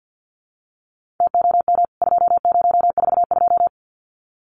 Morse code, heard as a keyed tone, about 50 wpm, 700 Hz. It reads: TOM 2042
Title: CWT Exchange 50wpm